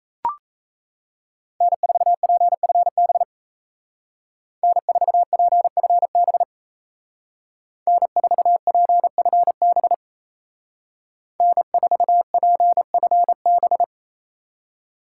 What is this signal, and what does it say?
Morse code, heard as a keyed tone, about 36 wpm, 700 Hz. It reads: N4PFB N4PFB N4PFB N4PFB